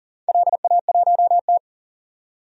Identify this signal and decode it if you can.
Morse code, heard as a keyed tone, about 40 wpm, 700 Hz. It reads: LA1T
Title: DXCC-100 Call Signs - 4 Characters - 40wpm